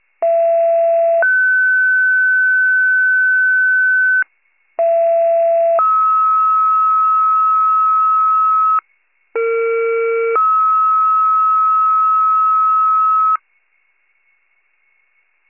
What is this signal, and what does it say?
Two-tone page, heard as a one-second tone, then a three-second tone, about 670 then 1500 Hz, about 670 then 1290 Hz, about 460 then 1290 Hz, with noise. It is heard from 0.2 to 13.4 s.
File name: wcalert